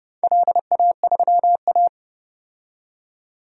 Morse code, heard as a keyed tone, about 32 wpm, 700 Hz. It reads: LA3A
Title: DXCC-100 Call Signs - 4 Characters - 30wpm